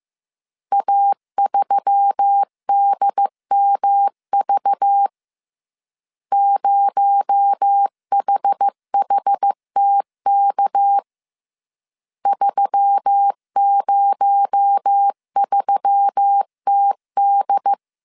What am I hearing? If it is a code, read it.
Morse code, heard as a keyed tone, about 15 wpm, 800 Hz. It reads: A3DMV 0HHTK 303TD